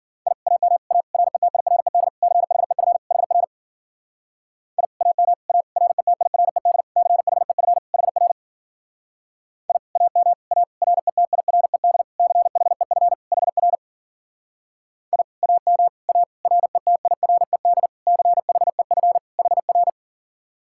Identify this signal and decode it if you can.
Morse code, heard as a keyed tone, about 52 wpm, 700 Hz. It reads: I AM A RETIRED CHEF HR I AM A RETIRED CHEF HR I AM A RETIRED CHEF HR I AM A RETIRED CHEF HR